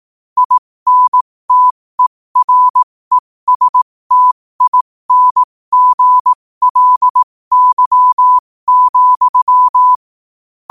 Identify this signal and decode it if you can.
Morse code, heard as a keyed tone, about 18 wpm, spaced as for 15 wpm, 1000 Hz. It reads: INTERESTINGLY,
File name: qotdfiles / 15wpm